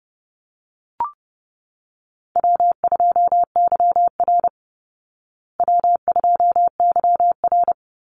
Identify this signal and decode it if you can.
Morse code, heard as a keyed tone, about 30 wpm, 700 Hz. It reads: W2YR W2YR
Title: Easy Call Signs Repeated 3x - 30wpm